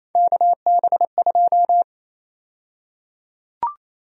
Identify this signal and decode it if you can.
Morse code, heard as a keyed tone, about 27 wpm, 700 Hz. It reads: KB2
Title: USA Call Sign Prefixes - Repeated 2x - 28wpm